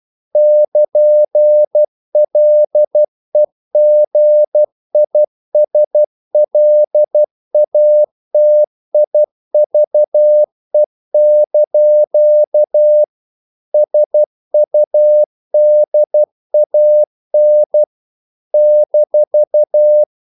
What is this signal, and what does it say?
Morse code, heard as a keyed tone, about 12 wpm, 600 Hz. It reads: <KN>LEGISLATIVE) SUDAN -
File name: CW / Practice / NationalCapitals12